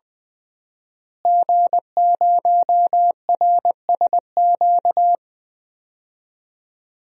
Morse code, heard as a keyed tone, about 21 wpm, 700 Hz. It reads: G0RSQ
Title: DXCC-100 Call Signs - 5 Characters - Repeated 3x - 20wpm